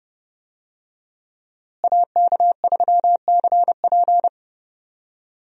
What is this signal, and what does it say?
Morse code, heard as a keyed tone, about 30 wpm, 700 Hz. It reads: AK3CP
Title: USA Call Signs - 5 Characters - Repeated 2x - 30wpm